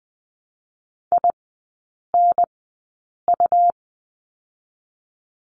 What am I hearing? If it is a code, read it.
Morse code, heard as a keyed tone, about 20 wpm, spaced as for 8 wpm, 700 Hz. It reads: INU
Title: 100 Lesson 17e - Letter u - Three Characters 20wpm